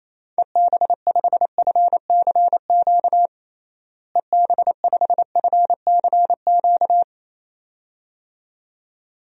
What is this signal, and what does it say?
Morse code, heard as a keyed tone, about 28 wpm, 700 Hz. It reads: EB5FCQ EB5FCQ
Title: DXCC-100 Call Signs - 6 Characters - Repeated 2x - 28wpm